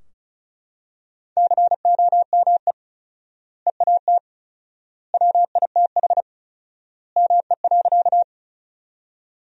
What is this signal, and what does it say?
Morse code, heard as a keyed tone, about 34 wpm, 700 Hz. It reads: COME EAT WITH ME.
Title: 4x Word-Spacing - Sentences from Top 300 Words - 35wpm